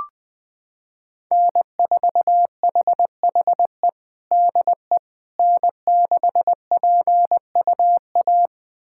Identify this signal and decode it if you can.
Morse code, heard as a keyed tone, about 20 wpm, 700 Hz. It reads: N4HHE DE N6PUA
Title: USA Callsign DE Callsign - 5 Characters - 20wpm